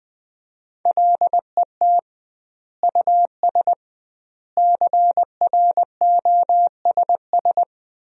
Morse code, heard as a keyed tone, about 20 wpm, 700 Hz. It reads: LET US CROSS
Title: 2x Word-Spacing - Sentences from Top 300 Words - 20wpm